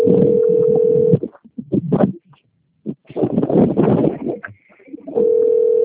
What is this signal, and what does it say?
Ringback tone (North America), 440 + 480 Hz.
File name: rg-1600-unknown- - - .69605